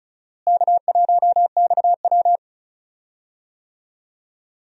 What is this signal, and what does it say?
Morse code, heard as a keyed tone, about 36 wpm, 700 Hz. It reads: K1XW
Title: USA Call Signs - 4 Characters - Repeated 2x - 35wpm